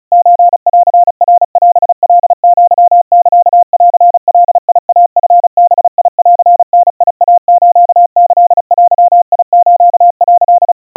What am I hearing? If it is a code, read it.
Morse code (Japanese Wabun code), heard as a keyed tone, about 36 wpm, 700 Hz. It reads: ソンナオカアサンガイチバンダイスキデス」
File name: kawa / Morse sample / kamyu 130 cpm 700 Hz Crispy